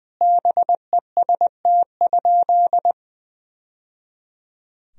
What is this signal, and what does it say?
Morse code, heard as a keyed tone, about 21 wpm, 700 Hz. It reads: BEST?
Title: Sentences from Top 1000 Words - Review Part 2 - 20wpm